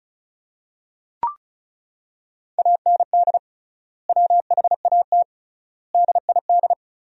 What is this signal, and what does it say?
Morse code, heard as a keyed tone, about 36 wpm, 700 Hz. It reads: AND WHAT DID
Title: 3x Word-Spacing - Sentences from Top 100 Words - 35wpm